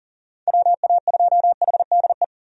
Morse code, heard as a keyed tone, about 37 wpm, 700 Hz. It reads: WA2HDE
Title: USA Call Signs - 6 Characters - 40wpm